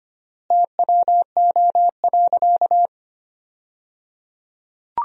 Morse code, heard as a keyed tone, about 25 wpm, 700 Hz. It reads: TWO.
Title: 3x Word-Spacing - Sentences from Top 200 Words - 25wpm